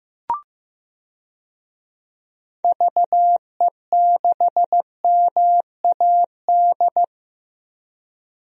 Morse code, heard as a keyed tone, about 15 wpm, 700 Hz. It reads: VE6MAD
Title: DXCC-100 Call Signs - 6 Characters - 15wpm